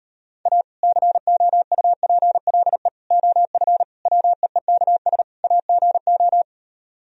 Morse code, heard as a keyed tone, about 38 wpm, 700 Hz. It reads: A COUPLE OF WEEKS AGO